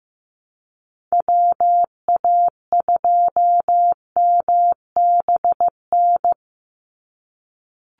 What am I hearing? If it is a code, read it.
Morse code, heard as a keyed tone, about 15 wpm, 700 Hz. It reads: WA2MBN
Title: USA Call Signs 15wpm